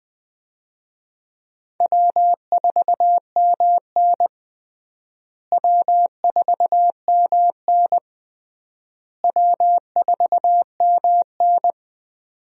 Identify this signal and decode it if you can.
Morse code, heard as a keyed tone, about 19 wpm, 700 Hz. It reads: W4MN W4MN W4MN
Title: USA Call Signs - 4 Characters - Repeated 3x - 20wpm